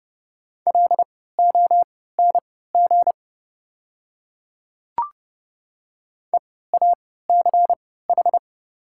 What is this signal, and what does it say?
Morse code, heard as a keyed tone, about 30 wpm, spaced as for 17 wpm, 700 Hz. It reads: LONG EACH
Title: 3x Character Spacing - Top 100 Words - 30wpm